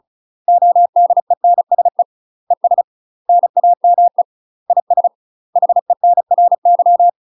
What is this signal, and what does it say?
Morse code, heard as a keyed tone, about 34 wpm, 700 Hz. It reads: ODENSE ES NAME IS HENRY
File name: UK MASTER QSOs TRAINING / 35WPM / UK MASTER QSOs 0042